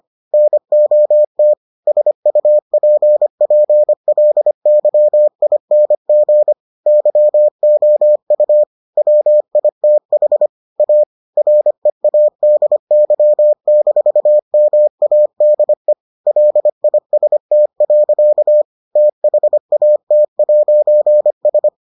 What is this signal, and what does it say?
Morse code, heard as a keyed tone, about 25 wpm, 600 Hz. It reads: NOT SUPPLYING YOU WITH A READY-MADE LIST. THAT'S